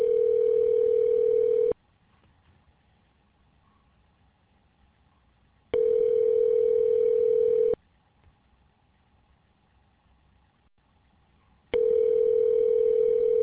Ringback tone (North America), 440 + 480 Hz.